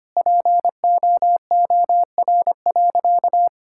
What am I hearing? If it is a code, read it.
Morse code, heard as a keyed tone, about 25 wpm, 700 Hz. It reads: POOR.
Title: Sentences from Top 1600 Words 25wpm